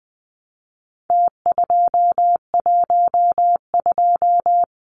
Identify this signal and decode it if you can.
Morse code, heard as a keyed tone, about 20 wpm, 700 Hz. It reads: T212
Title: Numbers - Contest Style - 4-Digit 20wpm